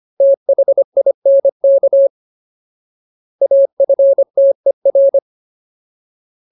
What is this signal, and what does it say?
Morse code, heard as a keyed tone, about 25 wpm, 550 Hz. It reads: THINK AFTER
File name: Ultimate Word List / 25WPM Ultimate